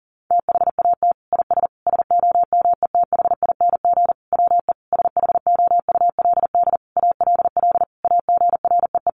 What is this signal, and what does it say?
Morse code, heard as a keyed tone, about 37 wpm, 700 Hz. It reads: THAT IS SOMETHING WE SHOULD ALL AGREE
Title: Sentences from Top 1100 Words 40wpm